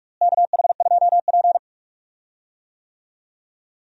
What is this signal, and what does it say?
Morse code, heard as a keyed tone, about 42 wpm, 700 Hz. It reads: KH2P
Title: USA Call Signs - 4 Characters - Repeated 2x - 45wpm